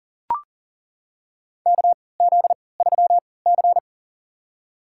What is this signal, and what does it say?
Morse code, heard as a keyed tone, about 40 wpm, spaced as for 23 wpm, 700 Hz. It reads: KZ3C
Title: USA Call Signs - 4 Characters - 3x Character Spacing - 40wpm